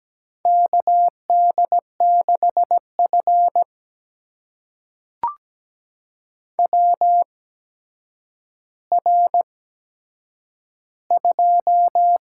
Morse code, heard as a keyed tone, about 17 wpm, spaced as for 8 wpm, 700 Hz. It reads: KD6F WR2
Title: USA Call Signs - 4 Characters - 8x Character Spacing - 17wpm